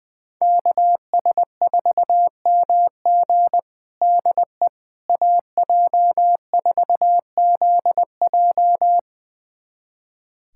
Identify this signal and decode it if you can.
Morse code, heard as a keyed tone, about 20 wpm, 700 Hz. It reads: KS4MG DE AJ4ZJ
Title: USA Callsign DE Callsign - 5 Characters - 20wpm